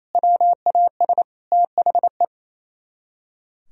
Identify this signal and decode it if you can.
Morse code, heard as a keyed tone, about 36 wpm, 700 Hz. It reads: WAS THE